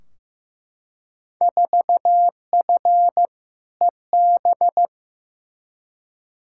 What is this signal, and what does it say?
Morse code, heard as a keyed tone, about 15 wpm, 700 Hz. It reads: 4F EB
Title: Field Day Exchanges - 15wpm